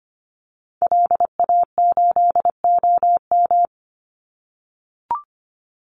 Morse code, heard as a keyed tone, about 25 wpm, 700 Hz. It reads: LA8OM
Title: Call Signs - 25wpm